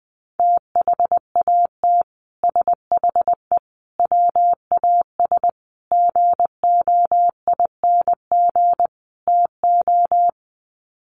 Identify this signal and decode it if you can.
Morse code, heard as a keyed tone, about 20 wpm, 700 Hz. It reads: THAT SHE WAS GOING TO
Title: Sets of 5 Words Encore2 20wpm